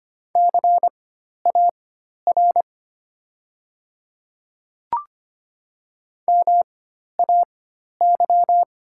Morse code, heard as a keyed tone, about 25 wpm, spaced as for 12 wpm, 700 Hz. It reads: CAR MAY
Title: 4x Character Spacing - Top 300 Words - 25wpm